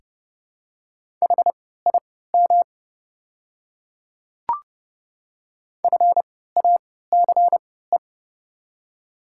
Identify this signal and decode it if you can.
Morse code, heard as a keyed tone, about 31 wpm, spaced as for 17 wpm, 700 Hz. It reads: HIM FACE
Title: 3x Character Spacing - Top 300 Words - 30wpm